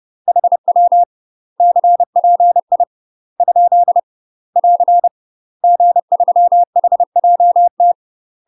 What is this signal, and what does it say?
Morse code, heard as a keyed tone, about 30 wpm, 700 Hz. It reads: HW CPI ? <AR> G3HJT
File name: UK MASTER QSOs TRAINING / 30WPM / UK MASTER QSOs 0016